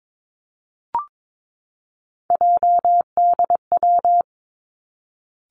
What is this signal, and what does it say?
Morse code, heard as a keyed tone, about 22 wpm, 700 Hz. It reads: JDW
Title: Three Challenging Letter-Number Combinations 22wpm